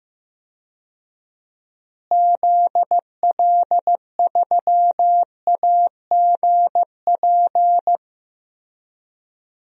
Morse code, heard as a keyed tone, about 15 wpm, 700 Hz. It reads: ZL3AGP
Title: DXCC-100 Call Signs - 6 Characters - Repeated 2x - 15wpm